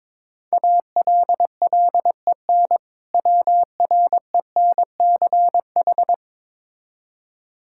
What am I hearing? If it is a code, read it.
Morse code, heard as a keyed tone, about 21 wpm, 700 Hz. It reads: ALLEN WRENCH
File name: Difficult Words with Hints - 22wpm